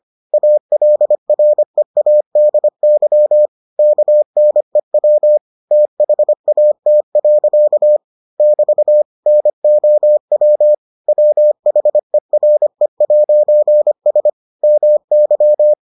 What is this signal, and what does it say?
Morse code, heard as a keyed tone, about 25 wpm, 600 Hz. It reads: ALREADY KNEW THAT. <BT> NOW WHERE'S MY